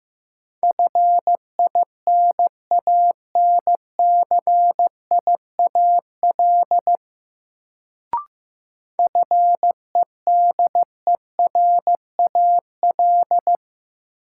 Morse code, heard as a keyed tone, about 15 wpm, 700 Hz. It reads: FINANCIAL FEDERAL
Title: Sentences from Top 1000 Words - Review Part 5 - New Word Drill - 15wpm